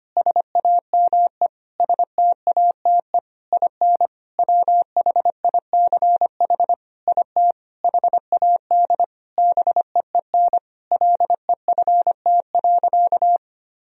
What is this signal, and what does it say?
Morse code, heard as a keyed tone, about 25 wpm, 700 Hz. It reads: SAME STATE IN WHICH IT HAD BEEN LEFT.
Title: Sentences from Top 1000 Words - Review Part 2 - 25wpm